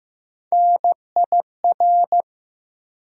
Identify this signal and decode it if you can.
Morse code, heard as a keyed tone, about 15 wpm, 700 Hz. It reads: NIR
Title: USA Call Sign Postfixes - 3 Letters 15wpm